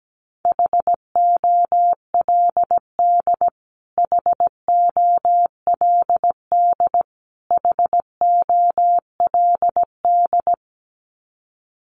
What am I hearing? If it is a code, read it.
Morse code, heard as a keyed tone, about 17 wpm, 700 Hz. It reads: HOLD HOLD HOLD
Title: Top 500 Words Repeated 17wpm